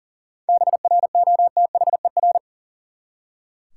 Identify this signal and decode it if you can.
Morse code, heard as a keyed tone, about 40 wpm, 700 Hz. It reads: BROTHER
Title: Seven Letter Words - 40wpm